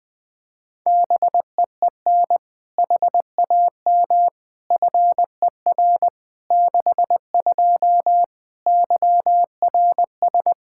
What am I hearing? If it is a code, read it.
Morse code, heard as a keyed tone, about 21 wpm, 700 Hz. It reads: BEEN HAM FER 62 YRS
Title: K1USN SST QSO Practice 20wpm